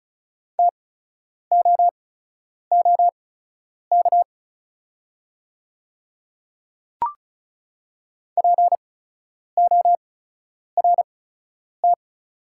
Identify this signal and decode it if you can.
Morse code, heard as a keyed tone, about 35 wpm, spaced as for 10 wpm, 690 Hz. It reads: TOOK PORT
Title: 8x Character Spacing - Top 300 Words - 35wpm